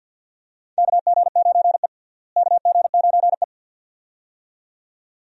Morse code, heard as a keyed tone, about 51 wpm, 700 Hz. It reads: KG9E KG9E
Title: USA Call Signs - 4 Characters - Repeated 2x - 50wpm